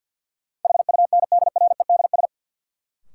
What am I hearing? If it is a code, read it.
Morse code, heard as a keyed tone, about 48 wpm, 700 Hz. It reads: HUNDREDS
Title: Sets of 4 Words - New Word Drill - 50wpm